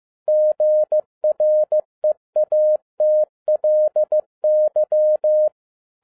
Morse code, heard as a keyed tone, about 15 wpm, 610 Hz. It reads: GREATLY
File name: EXCITING QRP CW OPERATION 15wpm